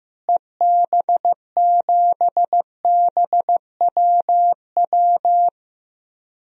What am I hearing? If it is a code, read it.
Morse code, heard as a keyed tone, about 15 wpm, 700 Hz. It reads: EB7BWW
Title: DXCC-100 Call Signs - 6 Characters - Repeated 3x - 15wpm